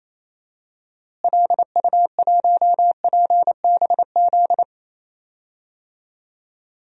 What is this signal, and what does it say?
Morse code, heard as a keyed tone, about 28 wpm, 700 Hz. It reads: LU1PBZ
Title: DXCC-100 Call Signs - 6 Characters - Repeated 2x - 28wpm